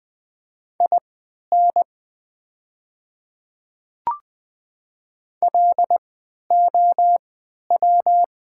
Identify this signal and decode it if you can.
Morse code, heard as a keyed tone, about 20 wpm, spaced as for 11 wpm, 690 Hz. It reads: IN LOW
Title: 3x Character Spacing - Top 200 Words - 20wpm